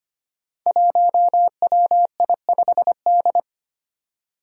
Morse code, heard as a keyed tone, about 25 wpm, 700 Hz. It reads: 1WI5D
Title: Five Challenging Letter-Number Combinations 25wpm